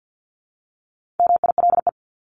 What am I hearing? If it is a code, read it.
Morse code, heard as a keyed tone, about 50 wpm, 700 Hz. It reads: MILE
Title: ES:S Suffix - Top 100 Words with Base Words 50wpm